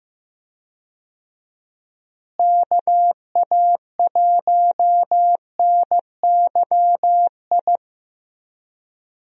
Morse code, heard as a keyed tone, about 15 wpm, 700 Hz. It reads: KA1NYI
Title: USA Call Signs - 6 Characters - Repeated 2x - 15wpm